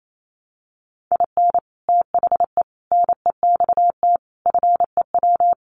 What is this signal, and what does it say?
Morse code, heard as a keyed tone, about 28 wpm, 690 Hz. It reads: IN THE NEXT FEW
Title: Sets of 5 Words Encore 28wpm